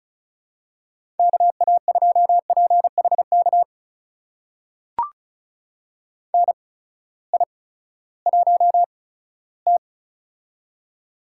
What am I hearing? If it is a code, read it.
Morse code, heard as a keyed tone, about 36 wpm, spaced as for 18 wpm, 700 Hz. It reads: KA2PHK NI1T
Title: USA Call Signs - 8x Character Spacing - 35wpm